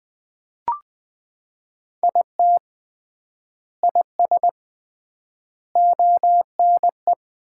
Morse code, heard as a keyed tone, about 20 wpm, 700 Hz. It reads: IT IS ONE
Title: 3x Word-Spacing - Sets of 5 Words - 20wpm